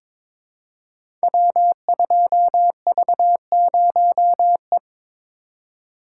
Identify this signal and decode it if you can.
Morse code, heard as a keyed tone, about 23 wpm, 700 Hz. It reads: W2V0E
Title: Code Groups - 5 Letter:Numbers - 22wpm